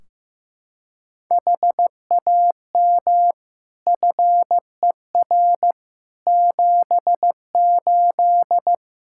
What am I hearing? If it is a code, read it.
Morse code, heard as a keyed tone, about 15 wpm, 700 Hz. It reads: HAM FER 78
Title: Practice QSO - Exchange 3 - More Info - 15wpm